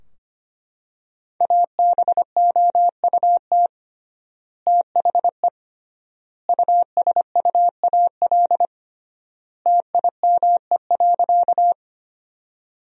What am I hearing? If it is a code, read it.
Morse code, heard as a keyed tone, about 25 wpm, 700 Hz. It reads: ABOUT THE USUAL TIME.
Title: 3x Word-Spacing - Sentences from Top 400 Words - 25wpm